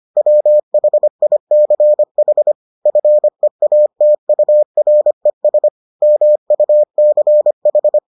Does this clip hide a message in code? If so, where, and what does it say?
Yes: Morse code from 0.2 to 8.0 s, heard as a keyed tone, about 25 wpm, 600 Hz. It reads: WHICH FEATURES MUCH